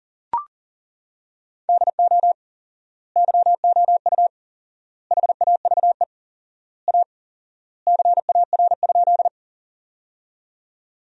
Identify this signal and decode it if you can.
Morse code, heard as a keyed tone, about 40 wpm, 700 Hz. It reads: DO YOU HAVE A CAR?
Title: 4x Word-Spacing - Sentences from Top 300 Words - 40wpm